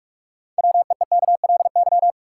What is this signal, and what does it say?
Morse code, heard as a keyed tone, about 46 wpm, 700 Hz. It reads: WEEKLY
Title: Six Letter Words - 45wpm